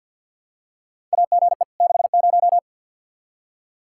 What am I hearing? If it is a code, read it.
Morse code, heard as a keyed tone, about 47 wpm, 700 Hz. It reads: AGE 60
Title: Practice QSO - Exchange 3 - More Info - 50wpm